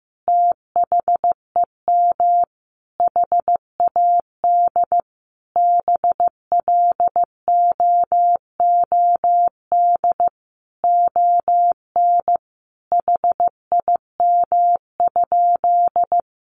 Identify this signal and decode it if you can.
Morse code, heard as a keyed tone, about 15 wpm, 700 Hz. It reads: THEM HAD BLOOD ON HIM?
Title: Sentences from Top 1100 Words 15wpm